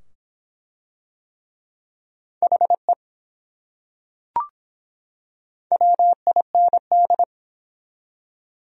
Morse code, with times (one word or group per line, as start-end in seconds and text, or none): 2.4-2.9 HE
5.7-7.2 WIND